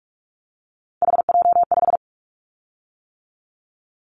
Morse code, heard as a keyed tone, about 46 wpm, 700 Hz. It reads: HJ5
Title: Callsign Prefixes for Top 100 DXCC - 45wpm